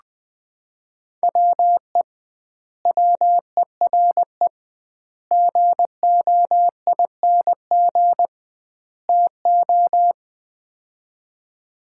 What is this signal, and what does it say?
Morse code, heard as a keyed tone, about 20 wpm, 700 Hz. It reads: WE WERE GOING TO
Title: 2x Word-Spacing - Sets of 4 Words - 20wpm